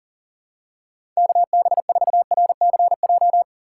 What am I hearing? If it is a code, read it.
Morse code, heard as a keyed tone, about 37 wpm, 700 Hz. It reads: KB4RCJ
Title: USA Call Signs - 4x Character Spacing - 40wpm